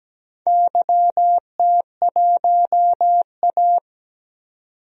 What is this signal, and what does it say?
Morse code, heard as a keyed tone, about 17 wpm, 700 Hz. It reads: YT1A
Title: DXCC-100 Call Signs - 4 Characters - 17wpm